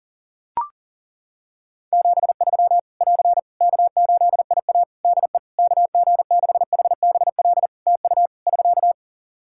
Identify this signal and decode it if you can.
Morse code, heard as a keyed tone, about 40 wpm, 700 Hz. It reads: 73 <AR> K8IA DE KG6HBL TU <SK>
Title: Practice QSO - Exchange 4 - Ending - 40wpm